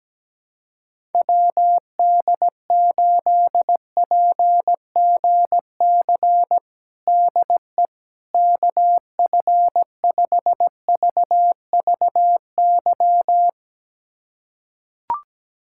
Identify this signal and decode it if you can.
Morse code, heard as a keyed tone, about 17 wpm, 700 Hz. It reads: WD8PGC DE KF5VVY
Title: USA Callsign DE Callsign - 6 Characters - 17wpm